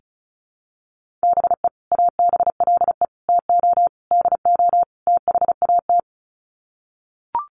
Morse code, heard as a keyed tone, about 34 wpm, 700 Hz. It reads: BE ABLE TO DO THAT
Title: Sets of 5 Words Encore2 35wpm